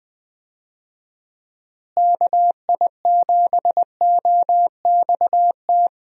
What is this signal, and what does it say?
Morse code, heard as a keyed tone, about 20 wpm, 700 Hz. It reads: KI7OXT
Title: USA Call Signs - 6 Characters - Repeated 2x - 20wpm